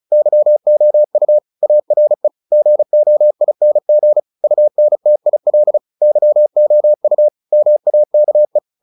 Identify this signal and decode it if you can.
Morse code, heard as a keyed tone, about 34 wpm, 600 Hz. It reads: YOU ARE GOING UNTIL YOU MAKE
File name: the old man and the sea / the old man and the sea 1 35wpm